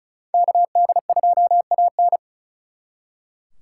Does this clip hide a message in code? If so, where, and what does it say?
Yes: Morse code from 0.3 to 2.2 s, heard as a keyed tone, about 44 wpm, 700 Hz. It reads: KD2AN